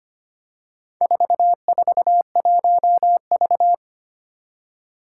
Morse code, heard as a keyed tone, about 25 wpm, 700 Hz. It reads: 441V
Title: Four Challenging Letter-Number Combinations 25wpm